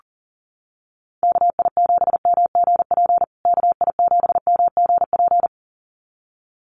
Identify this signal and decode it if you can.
Morse code, heard as a keyed tone, about 40 wpm, 700 Hz. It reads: KI7MGP KI7MGP
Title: USA Call Signs Repeated 2x 40wpm